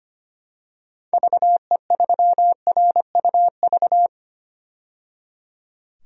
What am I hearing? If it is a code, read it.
Morse code, heard as a keyed tone, about 25 wpm, 700 Hz. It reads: VE3RUV
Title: DXCC-100 Call Signs - 6 Characters - 25wpm